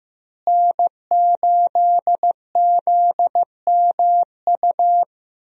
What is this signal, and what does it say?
Morse code, heard as a keyed tone, about 15 wpm, 700 Hz. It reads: N8ZMU
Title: USA Call Signs - 5 Characters - Repeated 3x - 15wpm